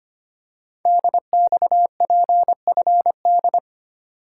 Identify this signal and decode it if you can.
Morse code, heard as a keyed tone, about 25 wpm, 700 Hz. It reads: DXPFD
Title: Code Groups - 5 Letters - 25wpm